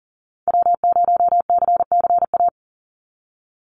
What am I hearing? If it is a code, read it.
Morse code, heard as a keyed tone, about 40 wpm, 700 Hz. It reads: W0CCA
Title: NA SOTA Activator Callsigns 40wpm